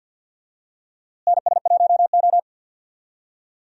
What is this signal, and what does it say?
Morse code, heard as a keyed tone, about 46 wpm, 700 Hz. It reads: NS1O
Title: USA Call Signs - 4 Characters - Repeated 3x - 50wpm